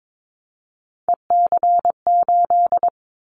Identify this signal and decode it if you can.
Morse code, heard as a keyed tone, about 22 wpm, 700 Hz. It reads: EC8
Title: Callsign Prefixes for Top 100 DXCC - 22wpm